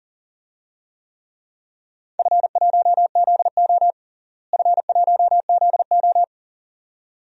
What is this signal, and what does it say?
Morse code, heard as a keyed tone, about 40 wpm, 700 Hz. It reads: F1ZO F1ZO
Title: DXCC-100 Call Signs - 4 Characters - Repeated 2x - 40wpm